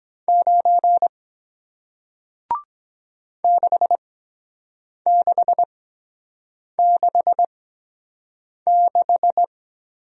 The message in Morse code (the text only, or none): 9 6666